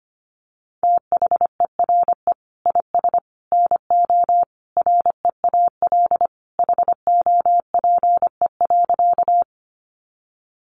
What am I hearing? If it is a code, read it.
Morse code, heard as a keyed tone, about 25 wpm, 700 Hz. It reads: THERE IS NO REAL HOPE.
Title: Sentences from Top 1000 Words 25wpm